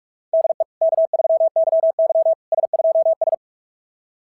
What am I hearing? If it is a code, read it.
Morse code, heard as a keyed tone, about 44 wpm, 690 Hz. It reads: DE K3YY S2S
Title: SOTA QSO Practice 45wpm